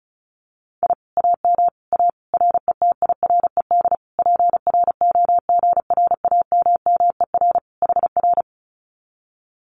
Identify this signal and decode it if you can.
Morse code, heard as a keyed tone, about 34 wpm, 700 Hz. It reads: I AM A RETIRED PROGRAMMER HR
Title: Common QSO Phrases Encore 35wpm